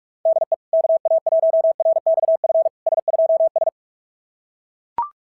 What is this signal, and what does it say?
Morse code, heard as a keyed tone, about 42 wpm, 660 Hz. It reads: DE KA1RXF S2S
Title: SOTA QSO Practice 45wpm